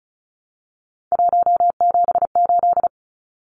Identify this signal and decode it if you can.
Morse code, heard as a keyed tone, about 34 wpm, 700 Hz. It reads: 178
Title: Numbers - 3-Digit 35wpm